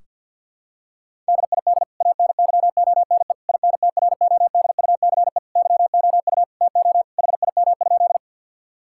Morse code, heard as a keyed tone, about 46 wpm, 700 Hz. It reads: DID ANYONE INTRODUCE YOU TO HIM?
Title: Sentences from Top 1600 Words 50wpm